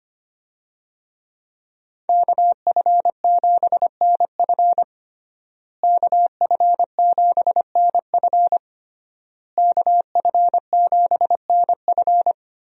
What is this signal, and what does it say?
Morse code, heard as a keyed tone, about 25 wpm, 700 Hz. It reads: KF7NF KF7NF KF7NF
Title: USA Call Signs - 5 Characters - Repeated 3x - 25wpm